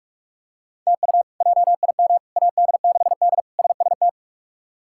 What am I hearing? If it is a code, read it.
Morse code, heard as a keyed tone, about 42 wpm, 700 Hz. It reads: TU JIM AD6D SST
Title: K1USN SST QSO Practice - 45wpm